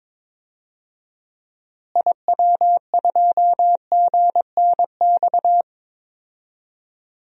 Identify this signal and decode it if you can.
Morse code, heard as a keyed tone, about 22 wpm, 700 Hz. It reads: IW2GNX
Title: DXCC-100 Call Signs - 6 Characters - Repeated 3x - 22wpm